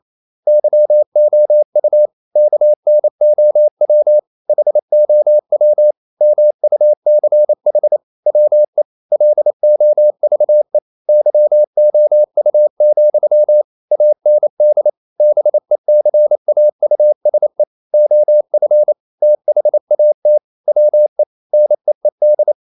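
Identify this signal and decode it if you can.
Morse code, heard as a keyed tone, about 28 wpm, 600 Hz. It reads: YOU KNOW HOW MUCH WE LOVE YOU, AND BECAUSE OF THAT WE NEED
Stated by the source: Morse code practice at 28 wpm, text of Ken Follett's novel Never in English, from CW med Gnister.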